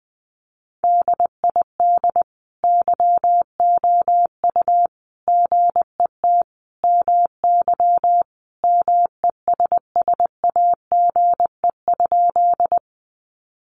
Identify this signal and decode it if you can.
Morse code, heard as a keyed tone, about 19 wpm, 700 Hz. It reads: DID YOU GET MY MESSAGE?
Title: Sentences from Top 1100 Words 20wpm